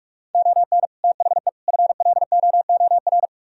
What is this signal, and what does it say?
Morse code, heard as a keyed tone, about 48 wpm, 700 Hz. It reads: ON THE FLOOR